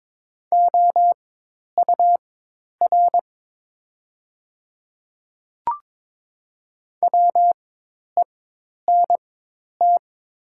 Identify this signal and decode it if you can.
Morse code, heard as a keyed tone, about 22 wpm, spaced as for 10 wpm, 700 Hz. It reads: OUR WENT
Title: 4x Character Spacing - Top 200 Words - 22wpm